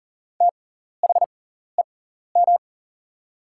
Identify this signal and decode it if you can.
Morse code, heard as a keyed tone, about 40 wpm, spaced as for 14 wpm, 690 Hz. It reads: THEM
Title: 6x Character Spacing - Top 300 Words - 40wpm